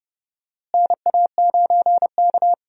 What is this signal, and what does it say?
Morse code, heard as a keyed tone, about 32 wpm, 700 Hz. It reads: NA9K
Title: USA Call Signs - 4 Characters - Repeated 2x - 30wpm